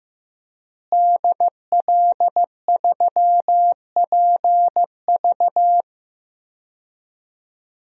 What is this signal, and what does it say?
Morse code, heard as a keyed tone, about 15 wpm, 700 Hz. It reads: DL3PV
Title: DXCC-100 Call Signs - 5 Characters - Repeated 2x - 15wpm